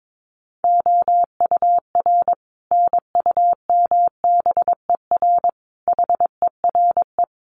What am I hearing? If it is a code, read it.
Morse code, heard as a keyed tone, about 22 wpm, 700 Hz. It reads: OUR NUMBER HERE
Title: Sets of 5 Words Encore2 22wpm